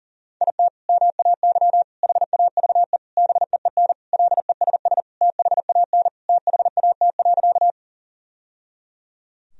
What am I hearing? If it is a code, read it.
Morse code, heard as a keyed tone, about 40 wpm, 700 Hz. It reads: IT MAY HAVE BEEN LESS THAN THAT.
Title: Sentences from Top 1000 Words - Review Part 2 - 40wpm